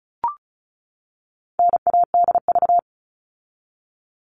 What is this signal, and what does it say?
Morse code, heard as a keyed tone, about 34 wpm, 700 Hz. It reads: NADV
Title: All Cut Numbers - 4-Digit 35wpm